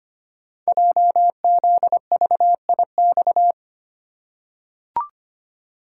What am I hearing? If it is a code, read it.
Morse code, heard as a keyed tone, about 25 wpm, 700 Hz. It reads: JZVIX
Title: Code Groups - 5 Letters - 25wpm